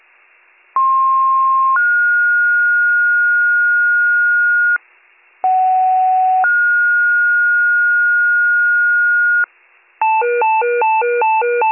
Two-tone page, heard as a one-second tone, then a three-second tone, about 1060 then 1430 Hz, about 750 then 1430 Hz. It is heard from 0.8 to 9.5 s.